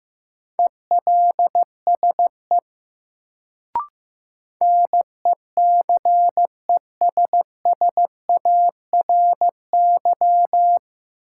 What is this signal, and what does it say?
Morse code, heard as a keyed tone, about 15 wpm, 700 Hz. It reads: ELSE NECESSARY
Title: Sentences from Top 800 Words - New Word Drill - 15wpm